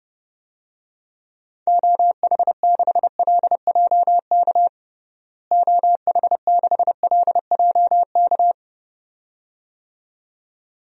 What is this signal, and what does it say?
Morse code, heard as a keyed tone, about 30 wpm, 700 Hz. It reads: OH6LJK OH6LJK
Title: DXCC-100 Call Signs - 6 Characters - Repeated 2x - 30wpm